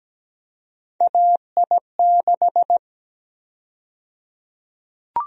Morse code, heard as a keyed tone, about 17 wpm, 700 Hz. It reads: AI6
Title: USA Call Sign Prefixes - 17wpm